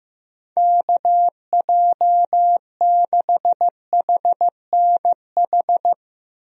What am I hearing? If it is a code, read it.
Morse code, heard as a keyed tone, about 15 wpm, 700 Hz. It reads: KJ6HNH
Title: USA Call Signs - 6 Characters - Repeated 3x - 15wpm